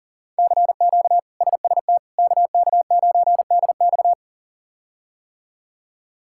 Morse code, heard as a keyed tone, about 37 wpm, 700 Hz. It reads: CQ SST KK9DX
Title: K1USN SST QSO Practice - 40wpm